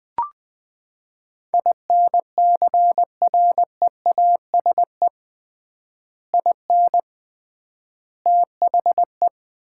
Morse code, heard as a keyed tone, about 21 wpm, 700 Hz. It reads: INCREASE IN THE
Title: 3x Word-Spacing - Sets of 5 Words - 20wpm